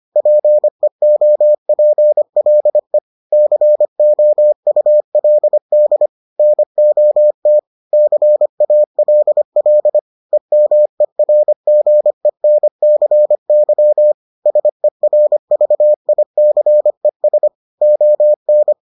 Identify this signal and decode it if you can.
Morse code, heard as a keyed tone, about 25 wpm, 600 Hz. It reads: PEOPLE COULD NOT CALL EMERGENCY SERVICES ON